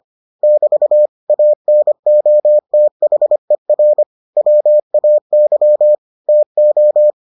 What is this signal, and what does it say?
Morse code, heard as a keyed tone, about 25 wpm, 600 Hz. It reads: <BT> ANOTHER WAY TO